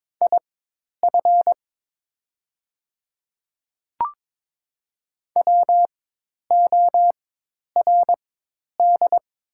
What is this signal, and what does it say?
Morse code, heard as a keyed tone, about 22 wpm, spaced as for 10 wpm, 700 Hz. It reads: IF WORD
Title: 4x Character Spacing - Top 200 Words - 22wpm